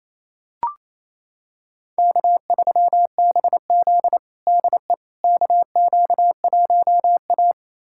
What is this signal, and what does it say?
Morse code, heard as a keyed tone, about 28 wpm, 700 Hz. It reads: K3BZ DE KQ1A
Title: USA Callsign DE Callsign - 4 Characters - 28wpm